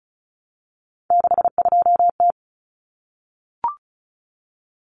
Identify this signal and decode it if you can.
Morse code, heard as a keyed tone, about 34 wpm, 700 Hz. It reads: 62T
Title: Common Cut Numbers - 3-Digit 35wpm